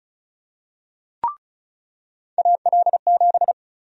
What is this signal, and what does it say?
Morse code, heard as a keyed tone, about 34 wpm, 700 Hz. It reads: AL7
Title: USA Call Sign Prefixes - 35wpm